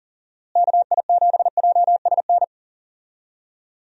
Morse code, heard as a keyed tone, about 40 wpm, 700 Hz. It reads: KI7JSN
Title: USA Call Signs - 6 Characters - Repeated 3x - 40wpm